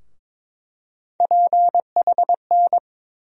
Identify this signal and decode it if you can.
Morse code, heard as a keyed tone, about 22 wpm, 700 Hz. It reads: PHN
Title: USA Call Sign Postfixes - 3 Letters 22wpm